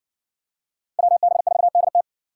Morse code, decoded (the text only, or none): WB4DT